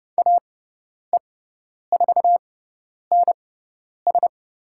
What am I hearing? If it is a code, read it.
Morse code, heard as a keyed tone, about 30 wpm, spaced as for 10 wpm, 700 Hz. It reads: AE4NS
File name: Lesson 18g - Letter m - Call Signs 30wpm